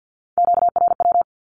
Morse code, heard as a keyed tone, about 50 wpm, 700 Hz. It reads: QRW
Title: Q-Codes with Spoken Meaning - 50wpm